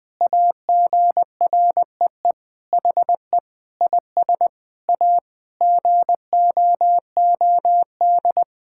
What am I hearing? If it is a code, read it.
Morse code, heard as a keyed tone, about 21 wpm, 700 Hz. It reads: AGREE HE IS A GOOD
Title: Sentences with 8-Word Phrases - 20wpm